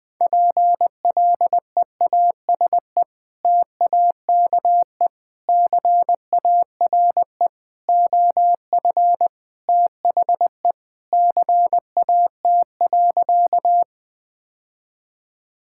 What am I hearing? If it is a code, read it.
Morse code, heard as a keyed tone, about 20 wpm, 700 Hz. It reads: PLEASE TAKE CARE OF THE CAT.
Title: Sentences with 3-Word Phrases - 20wpm